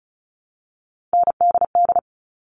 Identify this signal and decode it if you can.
Morse code, heard as a keyed tone, about 34 wpm, 700 Hz. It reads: NDD
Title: All Cut Numbers - 3-Digit 35wpm